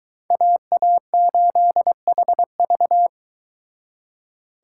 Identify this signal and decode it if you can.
Morse code, heard as a keyed tone, about 23 wpm, 700 Hz. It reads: AA8HV